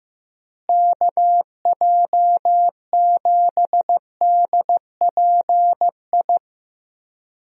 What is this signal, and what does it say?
Morse code, heard as a keyed tone, about 15 wpm, 700 Hz. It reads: KJ7DPI
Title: USA Call Signs - 6 Characters - Repeated 3x - 15wpm